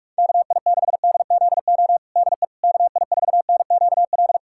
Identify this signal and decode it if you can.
Morse code, heard as a keyed tone, about 42 wpm, 700 Hz. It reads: KI6DZO DE KI4NQL
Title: USA Callsign DE Callsign 45wpm